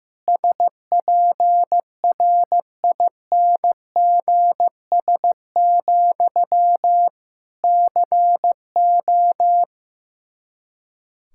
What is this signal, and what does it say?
Morse code, heard as a keyed tone, about 15 wpm, 700 Hz. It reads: SPRINGS, CO
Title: Practice QSO - Exchange 1 - Beginning - 15wpm